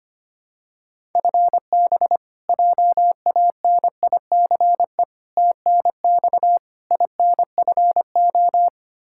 Morse code, heard as a keyed tone, about 25 wpm, 700 Hz. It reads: FB JANICE TNX INFO
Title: K1USN SST QSO Practice 25wpm